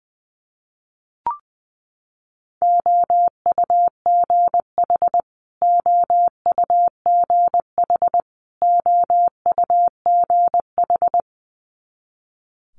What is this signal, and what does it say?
Morse code, heard as a keyed tone, about 20 wpm, 700 Hz. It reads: OUGH OUGH OUGH
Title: Multi-letter Phonemes Repeated 3x 20wpm